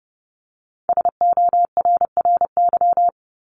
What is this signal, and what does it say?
Morse code, heard as a keyed tone, about 30 wpm, 700 Hz. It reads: SORRY
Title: Top 1751-2000 Words 30wpm